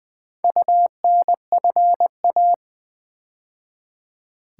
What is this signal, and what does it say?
Morse code, heard as a keyed tone, about 20 wpm, 700 Hz. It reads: UNFA
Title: Build A Word - 2 Letters At A Time - 6-Letter Words - 20wpm